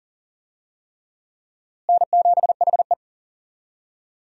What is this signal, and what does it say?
Morse code, heard as a keyed tone, about 40 wpm, 700 Hz. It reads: N7HE
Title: USA Call Signs - 4 Characters - Repeated 3x - 40wpm